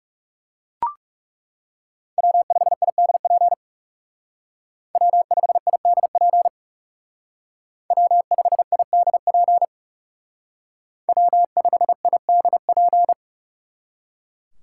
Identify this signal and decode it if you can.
Morse code, heard as a keyed tone, about 37 wpm, 700 Hz. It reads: W5IDP W5IDP W5IDP W5IDP
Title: Speed Racing - USA Call Signs 30wpm